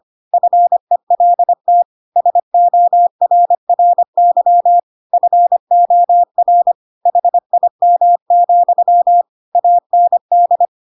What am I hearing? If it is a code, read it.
Morse code, heard as a keyed tone, about 25 wpm, 700 Hz. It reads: FELT SORRY FOR HIM, AND
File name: Pinocchio / Pinocchio ch7 25 25 ik1hge